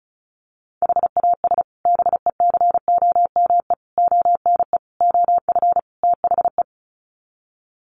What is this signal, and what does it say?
Morse code, heard as a keyed tone, about 34 wpm, 700 Hz. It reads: HAS BECOME ONE OF THE
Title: Sets of 5 Words Encore 35wpm